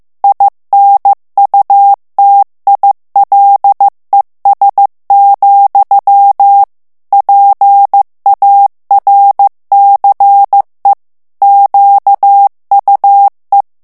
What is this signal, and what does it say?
Morse code, heard as a keyed tone, about 15 wpm, 800 Hz. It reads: INUTILES, PARCE QUE
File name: morse mp3 / FRANCAIS TEXTE CLAIR 2 VITESSE 14 MOTS MINUTES